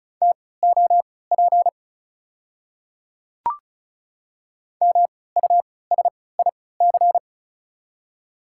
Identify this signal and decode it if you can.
Morse code, heard as a keyed tone, about 36 wpm, spaced as for 20 wpm, 690 Hz. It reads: TOP MUSIC
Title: 3x Character Spacing - Top 500 Words - 35wpm